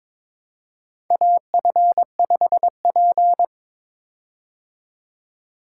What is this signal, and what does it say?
Morse code, heard as a keyed tone, about 22 wpm, 700 Hz. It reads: AF5P
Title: K1USN SST QSO Practice - 22wpm